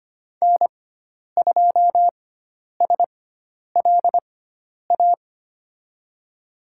Morse code, heard as a keyed tone, about 25 wpm, spaced as for 10 wpm, 700 Hz. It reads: N2SLA
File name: Lesson 16g - Letter c - Call Signs 25wpm